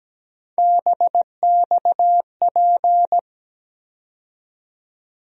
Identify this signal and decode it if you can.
Morse code, heard as a keyed tone, about 17 wpm, 700 Hz. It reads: BXP
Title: USA Call Sign Postfixes - 3 Letters 17wpm